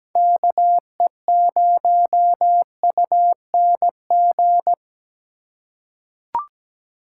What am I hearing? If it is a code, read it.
Morse code, heard as a keyed tone, about 17 wpm, 700 Hz. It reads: KE0UNG
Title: USA Call Signs - 6 Characters - 3x Character Spacing - 17wpm